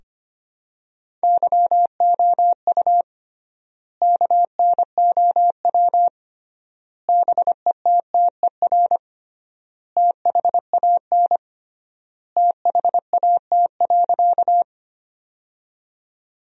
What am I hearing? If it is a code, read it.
Morse code, heard as a keyed tone, about 25 wpm, 700 Hz. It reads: YOU KNOW BETTER THAN THAT.
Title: 3x Word-Spacing - Sentences from Top 400 Words - 25wpm